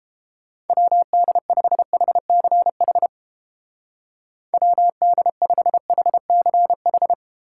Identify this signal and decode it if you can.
Morse code, heard as a keyed tone, about 32 wpm, 700 Hz. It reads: WD5HCH WD5HCH